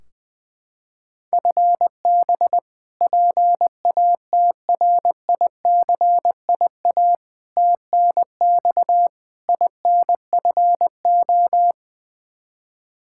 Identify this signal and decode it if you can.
Morse code, heard as a keyed tone, about 20 wpm, 700 Hz. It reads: FB PATRICIA TNX INFO
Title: Practice QSO - Exchange 3 - More Info - 20wpm